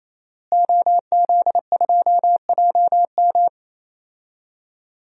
Morse code, heard as a keyed tone, about 28 wpm, 700 Hz. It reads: OZ2JM
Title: DXCC-100 Call Signs - 5 Characters - Repeated 3x - 28wpm